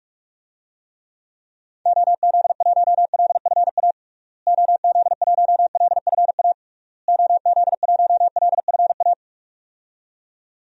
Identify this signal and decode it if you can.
Morse code, heard as a keyed tone, about 41 wpm, 700 Hz. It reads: OZ1LFA OZ1LFA OZ1LFA
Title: DXCC-100 Call Signs - 6 Characters - Repeated 3x - 45wpm